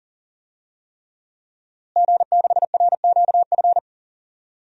Morse code, heard as a keyed tone, about 40 wpm, 700 Hz. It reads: G6RQF
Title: DXCC-100 Call Signs - 5 Characters - Repeated 3x - 40wpm